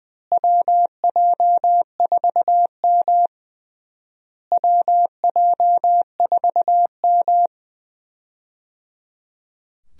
Morse code, heard as a keyed tone, about 20 wpm, 700 Hz. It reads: WJ4M WJ4M
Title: USA Call Signs - 4 Characters - Repeated 3x - 20wpm